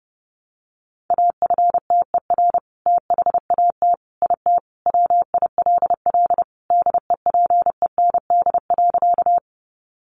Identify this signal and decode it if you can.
Morse code, heard as a keyed tone, about 30 wpm, 700 Hz. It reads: AFTER THAT IT WILL DEPEND.
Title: Sentences from Top 1300 Words 30wpm